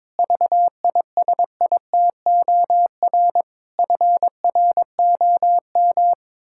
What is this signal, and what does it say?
Morse code, heard as a keyed tone, about 22 wpm, 700 Hz. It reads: VISITOR FROM
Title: Pangram Challenge v2 - 22wpm